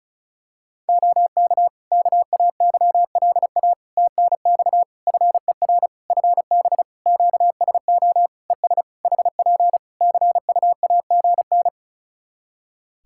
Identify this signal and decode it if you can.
Morse code, heard as a keyed tone, about 36 wpm, 700 Hz. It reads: OK KAYLA TNX FER FB QSO ES HP CUAGN
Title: K1USN SST QSO Practice 35wpm